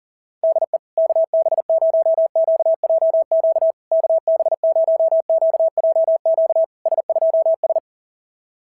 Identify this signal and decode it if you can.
Morse code, heard as a keyed tone, about 37 wpm, 650 Hz. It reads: DE KB0QJQ KB0QJQ S2S
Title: SOTA QSO Practice 40wpm